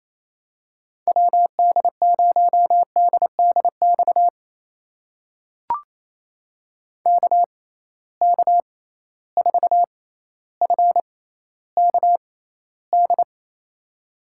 Morse code, heard as a keyed tone, about 28 wpm, spaced as for 14 wpm, 700 Hz. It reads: WD0DDX KK4FKD
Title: USA Call Signs - 6 Characters - 6x Character Spacing - 28wpm